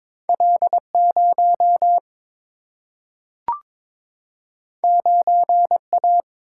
Morse code, heard as a keyed tone, about 22 wpm, 700 Hz. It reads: L 0 9 A
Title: Single-Letter Single-Number 22wpm